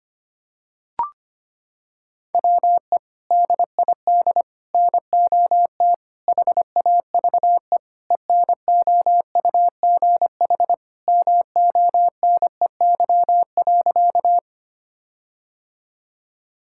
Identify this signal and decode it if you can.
Morse code, heard as a keyed tone, about 25 wpm, 700 Hz. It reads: WE DID NOT HAVE ENOUGH MONEY.
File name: practice / 25wpm / Sentences with 4-Word Phrases - 25wpm